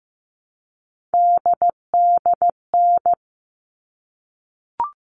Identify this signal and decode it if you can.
Morse code, heard as a keyed tone, about 15 wpm, 700 Hz. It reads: DDN
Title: All Cut Numbers - 3-Digit 15wpm